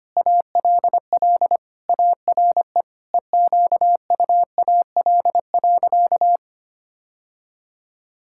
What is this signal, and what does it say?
Morse code, heard as a keyed tone, about 25 wpm, 700 Hz. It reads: ALL ARE EQUAL.
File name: practice / 25wpm / Sentences with 9-Word Phrases - 25wpm